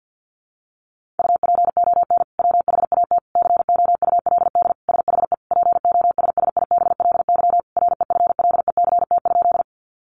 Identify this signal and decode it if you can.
Morse code, heard as a keyed tone, about 50 wpm, 700 Hz. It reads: UPON WHAT COULD SHE POSSIBLY REFLECT?
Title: Sentences from Top 1500 Words 50wpm